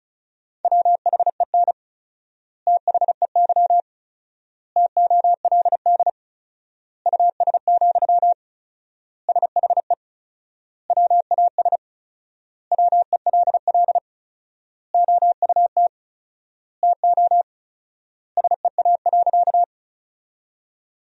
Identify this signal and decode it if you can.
Morse code, heard as a keyed tone, about 34 wpm, 700 Hz. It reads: WHEN THEY TOLD US, SHE WAS WELL OUT TO SEA.
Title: 4x Word-Spacing - Sentences from Top 400 Words - 35wpm